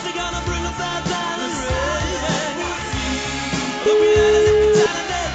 Ringback tone (Europe, 425 Hz), with noise.